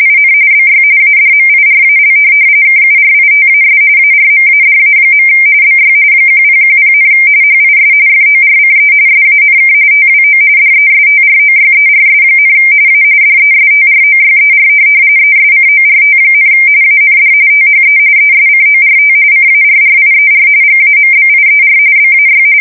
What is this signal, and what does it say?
RTTY, about 45.45 baud. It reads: IN BY ADMIRAL KURITAS ARMADA. BY 1010, HIS FORCE SANK TWO DESTROYERS, A DESTROYER ESCORT, AND THE ESCORT CARRIER GAMBIER BAY
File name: fsk170 / USS-New-Jersey